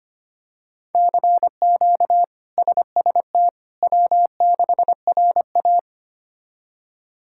Morse code, heard as a keyed tone, about 25 wpm, 700 Hz. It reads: CQ SST W6RA
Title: K1USN SST QSO Practice - 25wpm